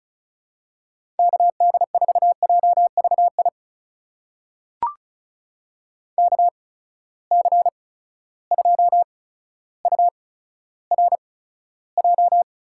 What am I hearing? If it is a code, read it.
Morse code, heard as a keyed tone, about 36 wpm, spaced as for 15 wpm, 700 Hz. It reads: KD4JVI KC2URJ
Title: USA Call Signs - 6 Characters - 8x Character Spacing - 35wpm